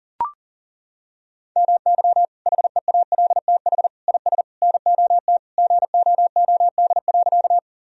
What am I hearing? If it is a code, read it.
Morse code, heard as a keyed tone, about 40 wpm, 700 Hz. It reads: MY HEALTH IS NOT GOOD.
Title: Sentences from Top 1000 Words - Review Part 3 - 40wpm